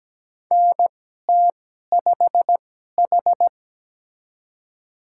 Morse code, heard as a keyed tone, about 17 wpm, spaced as for 12 wpm, 700 Hz. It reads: NT5H
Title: USA Call Signs - 4 Characters - 2x Character Spacing - 17wpm